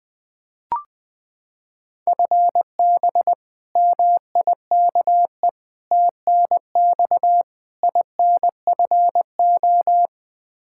Morse code, heard as a keyed tone, about 21 wpm, 700 Hz. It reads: FB MIKE TNX INFO
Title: Practice QSO - Exchange 3 - More Info - 20wpm